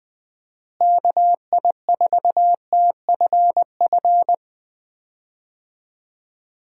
Morse code, heard as a keyed tone, about 20 wpm, 700 Hz. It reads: KI4TFF
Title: USA Call Signs - 6 Characters - Repeated 3x - 20wpm